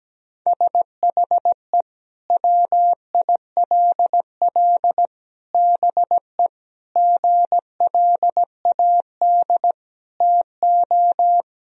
Morse code, heard as a keyed tone, about 17 wpm, 700 Hz. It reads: SHE WILL BE GLAD TO
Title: Sentences from Top 1000 Words - Review Part 2 - 17wpm